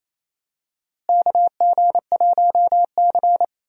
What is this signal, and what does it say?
Morse code, heard as a keyed tone, about 28 wpm, 700 Hz. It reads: KG1C
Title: USA Call Signs - 4 Characters - Repeated 3x - 28wpm